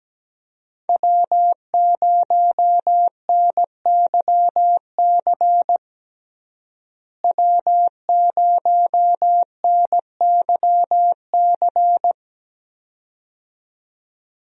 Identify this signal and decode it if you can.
Morse code, heard as a keyed tone, about 17 wpm, 700 Hz. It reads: W0NYC W0NYC
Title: USA Call Signs - 5 Characters - Repeated 2x - 17wpm